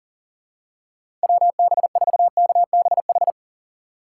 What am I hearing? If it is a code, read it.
Morse code, heard as a keyed tone, about 40 wpm, 700 Hz. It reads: WB4KBH
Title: USA Call Signs - 6 Characters - 6x Character Spacing - 40wpm